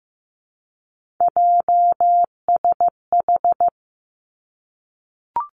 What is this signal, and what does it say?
Morse code, heard as a keyed tone, about 15 wpm, 700 Hz. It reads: JSH
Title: Three Challenging Letter-Number Combinations 15wpm